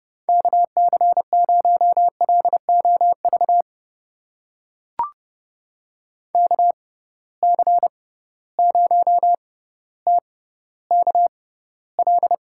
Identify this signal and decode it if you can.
Morse code, heard as a keyed tone, about 32 wpm, spaced as for 16 wpm, 700 Hz. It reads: KC0LOV KC0TKL
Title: USA Call Signs - 6 Characters - 6x Character Spacing - 30wpm